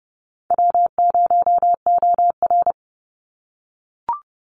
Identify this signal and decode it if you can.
Morse code, heard as a keyed tone, about 30 wpm, 700 Hz. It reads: W0OR
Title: Easy Call Signs - 30wpm